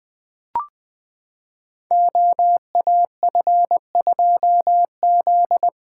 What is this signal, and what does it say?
Morse code, heard as a keyed tone, about 20 wpm, 700 Hz. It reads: OAF2Z
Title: Code Groups - 5 Letter:Numbers - 20wpm